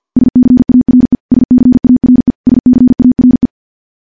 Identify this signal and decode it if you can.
Morse code, heard as a keyed tone, about 50 wpm, 250 Hz. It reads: SCARE SCARE SCARE
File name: morse / morse-training-50WPM